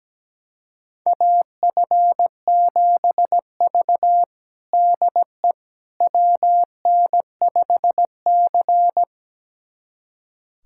Morse code, heard as a keyed tone, about 17 wpm, 700 Hz. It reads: AF7V DE WN5C
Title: USA Callsign DE Callsign - 4 Characters - 17wpm